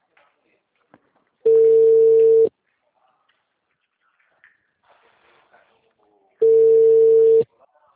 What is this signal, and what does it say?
Ringback tone (China, 450 Hz).
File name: out- -2012- - - .70720